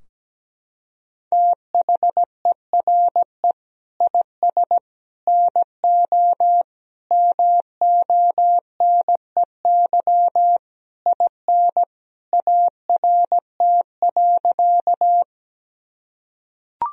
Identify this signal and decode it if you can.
Morse code, heard as a keyed tone, about 17 wpm, 700 Hz. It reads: THERE IS NO MONEY IN ART.
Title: Sentences from Top 1000 Words - Review Part 3 - 17wpm